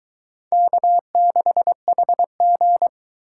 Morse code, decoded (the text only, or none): K6HG